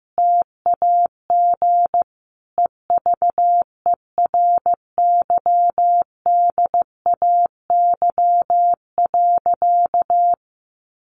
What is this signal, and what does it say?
Morse code, heard as a keyed tone, about 15 wpm, 700 Hz. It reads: TAG EVERYDAY.
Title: Additional Easy Phrases 15wpm